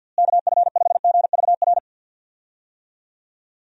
Morse code, heard as a keyed tone, about 48 wpm, 700 Hz. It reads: KF5GVR
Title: USA Call Signs - 6 Characters - Repeated 3x - 50wpm